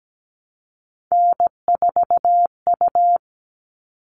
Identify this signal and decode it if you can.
Morse code, heard as a keyed tone, about 17 wpm, 700 Hz. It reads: N4U
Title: All Cut Numbers - 3-Digit 17wpm